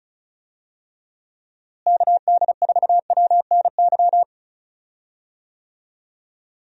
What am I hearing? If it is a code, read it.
Morse code, heard as a keyed tone, about 36 wpm, 700 Hz. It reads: KD4WNY
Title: USA Call Signs - 6 Characters - Repeated 2x - 35wpm